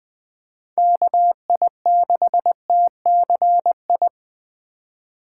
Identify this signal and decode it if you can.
Morse code, heard as a keyed tone, about 21 wpm, 700 Hz. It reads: KI6TCI
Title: USA Call Signs - 6 Characters - 3x Character Spacing - 20wpm